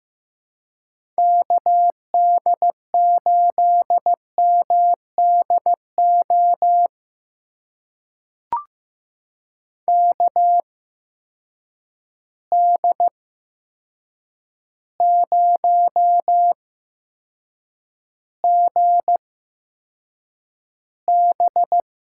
Morse code, heard as a keyed tone, about 15 wpm, spaced as for 7 wpm, 700 Hz. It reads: KD8MDO KD0GB
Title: USA Call Signs - 6 Characters - 8x Character Spacing - 15wpm